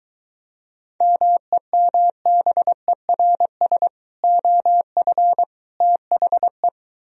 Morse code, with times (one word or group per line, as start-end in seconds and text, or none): 1.0-3.9 MEMBERS
4.2-5.4 OF
5.8-6.7 THE